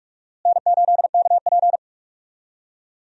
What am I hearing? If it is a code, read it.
Morse code, heard as a keyed tone, about 46 wpm, 700 Hz. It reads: N8KP
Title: USA Call Signs - 4 Characters - Repeated 2x - 45wpm